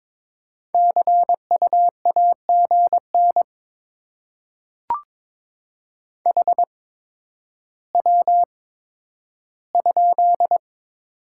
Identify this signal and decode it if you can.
Morse code, heard as a keyed tone, about 22 wpm, spaced as for 11 wpm, 700 Hz. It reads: CU AGN HW?
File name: Top 20 QSO Elements - 8x Character Spacing - 22wpm